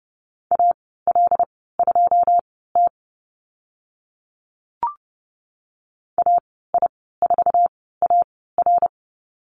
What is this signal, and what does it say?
Morse code, heard as a keyed tone, about 30 wpm, spaced as for 17 wpm, 700 Hz. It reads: AL2T AI4AR
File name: Lesson 14g - Number 2 - Call Signs 30wpm